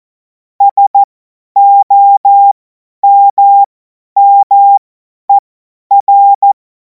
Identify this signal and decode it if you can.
Morse code, heard as a keyed tone, about 14 wpm, spaced as for 10 wpm, 800 Hz. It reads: SOMMER
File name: Morsekurs / Witze / Witz 14. Juli 2026 50 F70